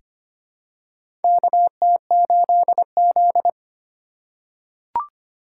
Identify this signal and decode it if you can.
Morse code, heard as a keyed tone, about 25 wpm, 700 Hz. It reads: KT8Z
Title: USA Call Signs - 4 Characters - 2x Character Spacing - 25wpm